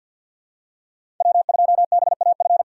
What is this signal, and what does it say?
Morse code, heard as a keyed tone, about 46 wpm, 700 Hz. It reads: W2BAF
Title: USA Call Signs - 5 Characters - 2x Character Spacing - 50wpm